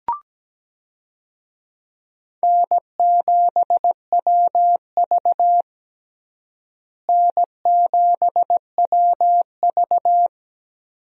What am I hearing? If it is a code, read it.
Morse code, heard as a keyed tone, about 17 wpm, 700 Hz. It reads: N7WV N7WV
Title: USA Call Signs - 4 Characters - Repeated 2x - 17wpm